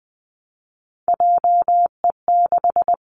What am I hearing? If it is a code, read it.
Morse code, heard as a keyed tone, about 20 wpm, 700 Hz. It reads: JE6
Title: Three Challenging Letter-Number Combinations 20wpm